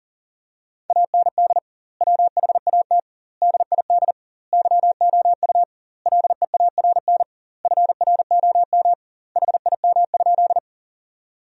Morse code, heard as a keyed tone, about 40 wpm, 700 Hz. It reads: AND WHAT DID YOU LEARN FROM HIM?
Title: 2x Word-Spacing - Sentences from Top 300 Words - 40wpm